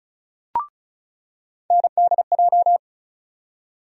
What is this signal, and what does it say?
Morse code, heard as a keyed tone, about 36 wpm, 700 Hz. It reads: NDJ
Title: USA Call Sign Postfixes - 3 Letters 35wpm